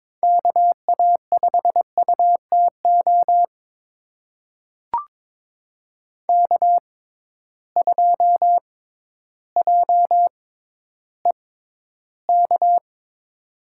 Morse code, heard as a keyed tone, about 22 wpm, spaced as for 12 wpm, 700 Hz. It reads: KA5UTO K2JEK
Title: USA Call Signs - 6x Character Spacing - 22wpm